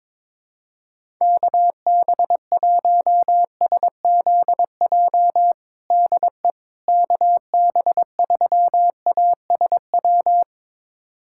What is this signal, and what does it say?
Morse code, heard as a keyed tone, about 22 wpm, 700 Hz. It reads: KB1SZJ DE KB3ASW
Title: USA Callsign DE Callsign - 6 Characters - 22wpm